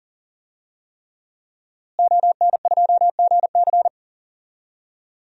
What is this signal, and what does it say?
Morse code, heard as a keyed tone, about 40 wpm, 700 Hz. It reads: ON2GC
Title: DXCC-100 Call Signs - 5 Characters - Repeated 3x - 40wpm